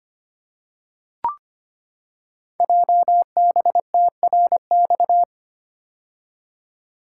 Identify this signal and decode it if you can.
Morse code, heard as a keyed tone, about 25 wpm, 700 Hz. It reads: JBTRX
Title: Code Groups - 5 Letters - 25wpm